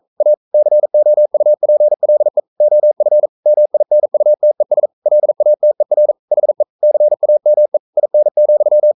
Morse code, heard as a keyed tone, about 42 wpm, 600 Hz. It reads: A COUPLE OF MINUTES LATER HE CAME IN,
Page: Romanen Never af Ken Follet på engelsk.